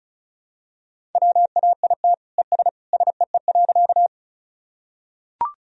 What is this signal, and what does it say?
Morse code, heard as a keyed tone, about 36 wpm, 700 Hz. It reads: WAIT ES SEE.
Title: Sentences from Top 500 Words and Abbreviations 35wpm